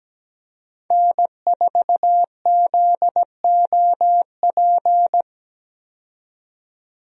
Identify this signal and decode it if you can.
Morse code, heard as a keyed tone, about 17 wpm, 700 Hz. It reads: N4ZOP
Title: USA Call Signs - 5 Characters - 17wpm